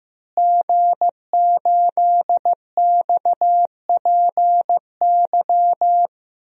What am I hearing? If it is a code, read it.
Morse code, heard as a keyed tone, about 15 wpm, 700 Hz. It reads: G8XPY
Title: DXCC-100 Call Signs - 5 Characters - Repeated 3x - 15wpm